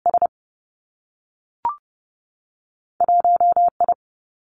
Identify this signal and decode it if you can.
Morse code, heard as a keyed tone, about 30 wpm, 700 Hz. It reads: S 1 I
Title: Two Challenging Letter-Number Combinations 30wpm